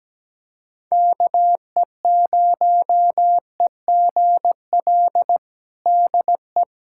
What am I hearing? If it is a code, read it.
Morse code, heard as a keyed tone, about 17 wpm, 700 Hz. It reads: KE0EGL DE
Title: USA Callsign DE Callsign 17wpm